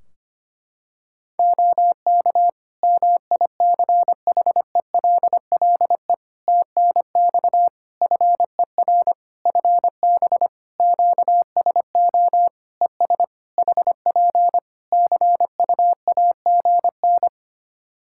Morse code, heard as a keyed tone, about 25 wpm, 700 Hz. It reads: OK MICHELLE TNX FER FB QSO ES HP CUAGN
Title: Practice QSO - Exchange 4 - Ending - 25wpm